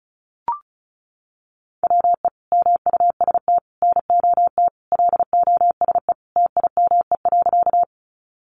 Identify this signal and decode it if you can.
Morse code, heard as a keyed tone, about 34 wpm, 700 Hz. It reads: WE MUST NOT LOSE TIME.
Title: Sentences from Top 1100 Words 35wpm